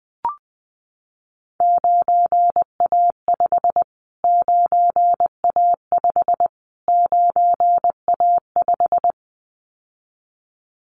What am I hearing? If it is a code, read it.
Morse code, heard as a keyed tone, about 20 wpm, 700 Hz. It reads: 9A5 9A5 9A5
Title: Callsign Prefixes for Top 25 DXCC - Repeated 3x - 20wpm